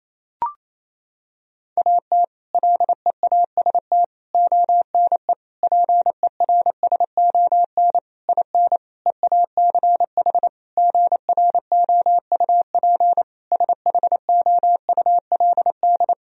Morse code, heard as a keyed tone, about 28 wpm, 700 Hz. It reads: AT LEAST ONE PERSON IN EACH GROUP SHOULD
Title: Sentences from Top 1000 Words - Review Part 1 - 28wpm